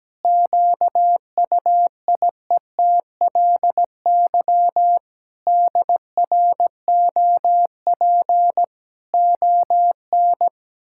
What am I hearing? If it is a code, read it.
Morse code, heard as a keyed tone, about 17 wpm, 700 Hz. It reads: QUIETLY DROP ON
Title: Pangram Challenge v2 - 17wpm